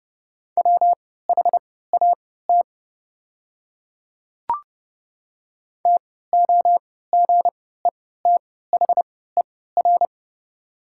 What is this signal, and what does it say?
Morse code, heard as a keyed tone, about 30 wpm, spaced as for 17 wpm, 700 Hz. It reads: WHAT TOGETHER
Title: 3x Character Spacing - Top 300 Words - 30wpm